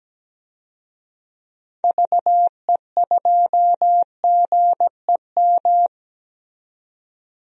Morse code, heard as a keyed tone, about 17 wpm, 700 Hz. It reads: VE2GEM
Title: DXCC-100 Call Signs - 6 Characters - 17wpm